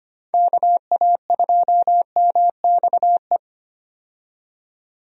Morse code, heard as a keyed tone, about 25 wpm, 700 Hz. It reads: KA2MXE
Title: USA Call Signs - 6 Characters - 25wpm